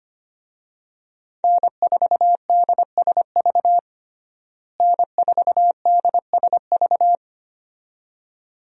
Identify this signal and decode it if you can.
Morse code, heard as a keyed tone, about 25 wpm, 700 Hz. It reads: N4DSV N4DSV
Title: USA Call Signs - 5 Characters - Repeated 2x - 25wpm